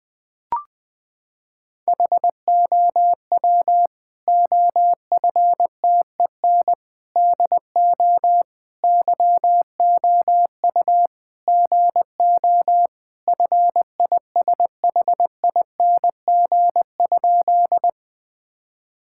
Morse code, heard as a keyed tone, about 20 wpm, 700 Hz. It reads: HOW OFTEN DO YOU GO FISHING?
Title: Sentences from Top 1000 Words - Review Part 5 - 20wpm